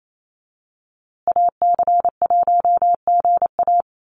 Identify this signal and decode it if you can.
Morse code, heard as a keyed tone, about 28 wpm, 700 Hz. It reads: AC1GA
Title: USA Call Signs Repeated 2x 28wpm